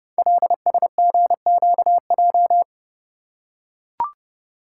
Morse code, heard as a keyed tone, about 30 wpm, 700 Hz. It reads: LSGQJ
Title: Code Groups - 5 Letter:Numbers - 30wpm